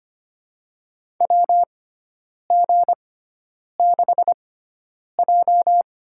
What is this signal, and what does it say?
Morse code, heard as a keyed tone, about 25 wpm, spaced as for 9 wpm, 700 Hz. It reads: WG6J
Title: USA Call Signs - 4 Characters - 6x Character Spacing - 25wpm